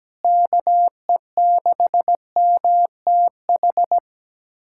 Morse code, heard as a keyed tone, about 17 wpm, 700 Hz. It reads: KE6MTH
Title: USA Callsign DE Callsign - 6 Characters - 17wpm